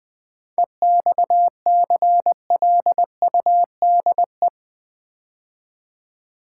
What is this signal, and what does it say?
Morse code, heard as a keyed tone, about 20 wpm, 690 Hz. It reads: EXCLUDE
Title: 7 Letter Words - First 3 Letters Given - 20wpm